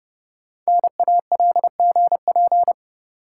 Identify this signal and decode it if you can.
Morse code, heard as a keyed tone, about 30 wpm, 700 Hz. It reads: NALGP
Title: Code Groups - 5 Letters - 30wpm